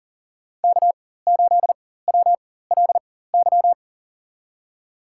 Morse code, heard as a keyed tone, about 40 wpm, spaced as for 19 wpm, 700 Hz. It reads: K8WLY
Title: USA Call Signs - 5 Characters - 4x Character Spacing - 40wpm